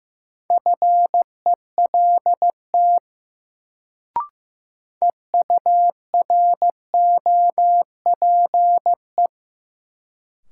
Morse code, heard as a keyed tone, about 15 wpm, 700 Hz. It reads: FELT EUROPE
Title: Sets of 5 Words Encore - New Word Drill - 15wpm